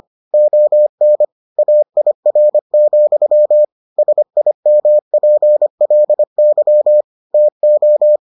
Morse code, heard as a keyed tone, about 25 wpm, 600 Hz. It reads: ON AIR, SIMPLY TO